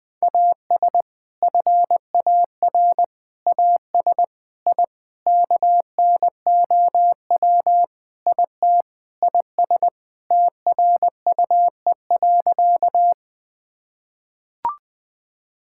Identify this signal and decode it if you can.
Morse code, heard as a keyed tone, about 21 wpm, 700 Hz. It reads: AS FAR AS I KNOW IT IS TRUE.
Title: Sentences with 4-Word Phrases - 20wpm